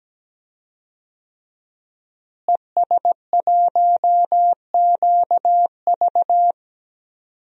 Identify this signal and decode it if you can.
Morse code, heard as a keyed tone, about 17 wpm, 700 Hz. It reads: ES1QV
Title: DXCC-100 Call Signs - 5 Characters - Repeated 3x - 17wpm